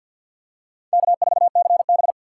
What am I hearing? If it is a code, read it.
Morse code, heard as a keyed tone, about 52 wpm, 700 Hz. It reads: K4CB
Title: USA Call Signs - 4 Characters - 2x Character Spacing - 50wpm